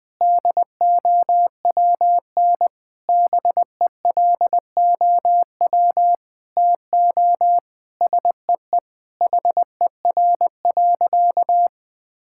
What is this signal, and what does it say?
Morse code, heard as a keyed tone, about 21 wpm, 700 Hz. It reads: DOWN BELOW TO SEE HER.
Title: Sentences from Top 1000 Words - Review Part 5 - 20wpm